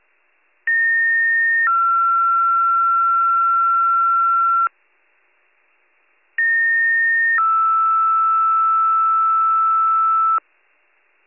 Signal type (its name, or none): two-tone page